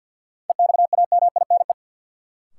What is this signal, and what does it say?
Morse code, heard as a keyed tone, about 49 wpm, 700 Hz. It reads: EXAMINE
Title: Sentences from Top 1600 Words - New Word Drill - 50wpm